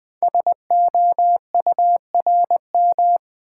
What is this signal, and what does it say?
Morse code, heard as a keyed tone, about 20 wpm, 700 Hz. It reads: SOURM
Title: Code Groups - 5 Letters - 20wpm